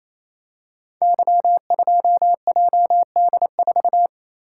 Morse code, heard as a keyed tone, about 27 wpm, 700 Hz. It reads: Y2JD4
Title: Code Groups - 5 Letter:Numbers - 28wpm